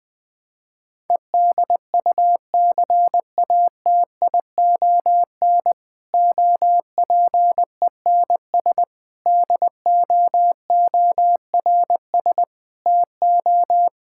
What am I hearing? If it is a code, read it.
Morse code, heard as a keyed tone, about 20 wpm, 700 Hz. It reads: EDUCATION OPENS DOORS TO
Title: Difficult Sentences with Hints - 20wpm